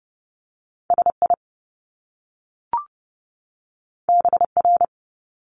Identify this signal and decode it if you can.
Morse code, heard as a keyed tone, about 30 wpm, 700 Hz. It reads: SI BR
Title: Top Two Letter Combinations in Words 30wpm